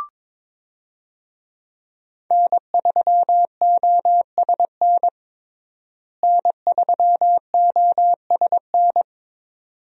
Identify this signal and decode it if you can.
Morse code, heard as a keyed tone, about 21 wpm, 700 Hz. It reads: N3OSN N3OSN
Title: USA Call Signs - 5 Characters - Repeated 2x - 22wpm